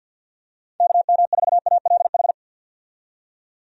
Morse code, heard as a keyed tone, about 48 wpm, 700 Hz. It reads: KM4ALH
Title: USA Call Signs - 6 Characters - 50wpm